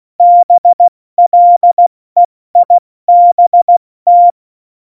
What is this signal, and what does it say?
Morse code, heard as a keyed tone, about 16 wpm, spaced as for 14 wpm, 700 Hz. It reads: BLEIBT
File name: Morsekurs / Witze / Witz 26 Mai, 2020 70 F80